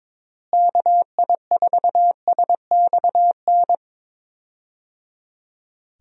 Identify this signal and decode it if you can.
Morse code, heard as a keyed tone, about 21 wpm, 700 Hz. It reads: KI4SXN
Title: USA Call Signs - 6 Characters - Repeated 3x - 22wpm